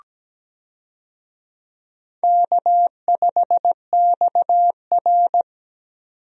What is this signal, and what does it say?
Morse code, heard as a keyed tone, about 17 wpm, 700 Hz. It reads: K5XR
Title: USA Call Signs - 4 Characters - 17wpm